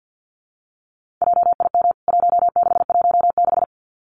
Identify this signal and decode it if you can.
Morse code, heard as a keyed tone, about 48 wpm, 700 Hz. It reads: JIM 1616
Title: CWT Exchange 50wpm